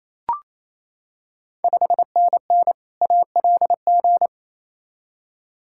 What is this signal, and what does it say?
Morse code, heard as a keyed tone, about 28 wpm, 700 Hz. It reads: 5NN ALG
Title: Ontario QSO Party Exchange 28wpm